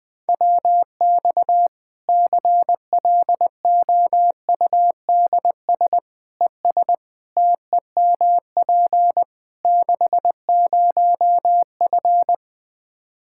Morse code, heard as a keyed tone, about 20 wpm, 700 Hz. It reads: WX CLOUDS ES TEMP 60F
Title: Practice QSO - Exchange 2 - Info - 20wpm